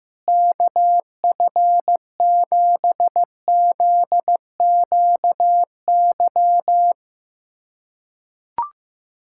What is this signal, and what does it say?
Morse code, heard as a keyed tone, about 15 wpm, 700 Hz. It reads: KF7ZQY
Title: USA Callsign DE Callsign - 6 Characters - 15wpm